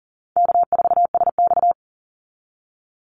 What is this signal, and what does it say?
Morse code, heard as a keyed tone, about 40 wpm, 700 Hz. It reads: K4SX
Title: Easy Call Signs Repeated 3x - 40wpm